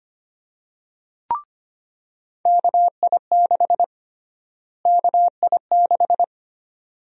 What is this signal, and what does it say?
Morse code, heard as a keyed tone, about 25 wpm, 700 Hz. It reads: KI6 KI6
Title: USA Call Sign Prefixes - Repeated 2x - 25wpm